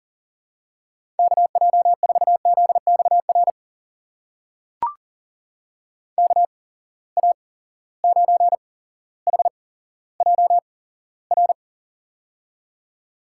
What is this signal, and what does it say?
Morse code, heard as a keyed tone, about 39 wpm, spaced as for 17 wpm, 700 Hz. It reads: KJ4ZXR KA9HJR
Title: USA Call Signs - 6 Characters - 8x Character Spacing - 40wpm